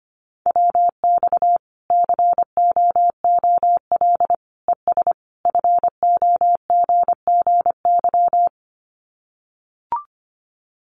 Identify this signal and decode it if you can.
Morse code, heard as a keyed tone, about 25 wpm, 700 Hz. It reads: WX COOL ES FOGGY
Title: Common QSO Phrases 25wpm